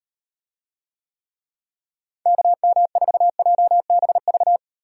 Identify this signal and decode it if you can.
Morse code, heard as a keyed tone, about 38 wpm, 700 Hz. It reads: KM4JBV